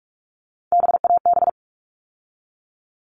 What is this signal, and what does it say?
Morse code, heard as a keyed tone, about 44 wpm, 700 Hz. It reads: BAB
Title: All Cut Numbers - 3-Digit 45wpm